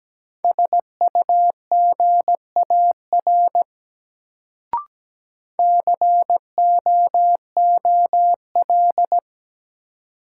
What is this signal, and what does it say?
Morse code, heard as a keyed tone, about 17 wpm, 700 Hz. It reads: SUGAR COOL
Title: Sentences from Top 1200 Words - New Word Drill Alt - 17wpm